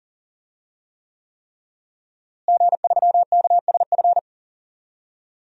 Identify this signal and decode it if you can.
Morse code, heard as a keyed tone, about 40 wpm, 700 Hz. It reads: G3KSF
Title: DXCC-100 Call Signs - 5 Characters - Repeated 3x - 40wpm